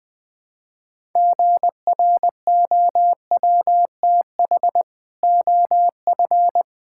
Morse code, heard as a keyed tone, about 21 wpm, 700 Hz. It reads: GROWTH OF
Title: Sentences from Top 1000 Words - Review Part 5 - 20wpm